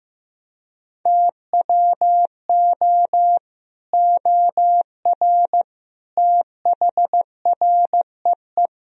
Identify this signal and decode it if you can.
Morse code, heard as a keyed tone, about 15 wpm, 700 Hz. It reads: TWO OR THREE
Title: Sentences from Top 1000 Words - Review Part 3 - 15wpm